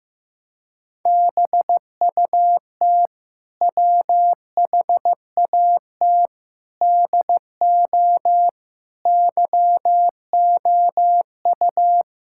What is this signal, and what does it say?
Morse code, heard as a keyed tone, about 16 wpm, 700 Hz. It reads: BUT WHAT DO YOU
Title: Sentences from Top 1600 Words 15wpm